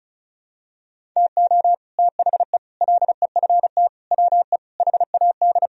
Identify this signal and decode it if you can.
Morse code, heard as a keyed tone, about 36 wpm, 700 Hz. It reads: TO THE LEFT WE HAD
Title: Sentences from Top 1000 Words - Review Part 1 - 35wpm